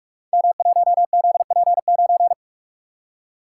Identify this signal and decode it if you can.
Morse code, heard as a keyed tone, about 44 wpm, 700 Hz. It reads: M1ZP9
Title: Code Groups - 5 Letter:Numbers - 45wpm